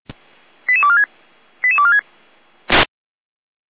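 Five-tone selective call (ZVEI), heard as quick runs of five tones, tones of about 70 ms. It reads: 88246 88246
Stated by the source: ZVEI five-tone call received by SDR, logged by the Orlamünde fire brigade (feuerwehr-orlamuende).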